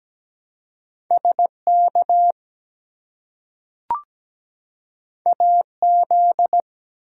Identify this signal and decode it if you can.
Morse code, heard as a keyed tone, about 17 wpm, 700 Hz. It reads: SK AZ
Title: US States and Canadian Provinces 17wpm